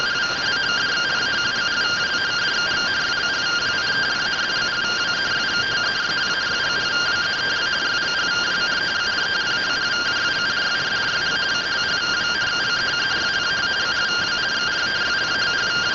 Two-tone teleprinter signal, about 170 Hz shift.